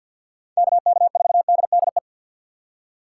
Morse code, heard as a keyed tone, about 52 wpm, 700 Hz. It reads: KK4DDE
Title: USA Call Signs - 6x Character Spacing - 50wpm